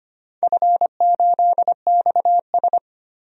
Morse code, heard as a keyed tone, about 25 wpm, 700 Hz. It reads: F8XS
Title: DXCC-100 Call Signs - 4 Characters - 25wpm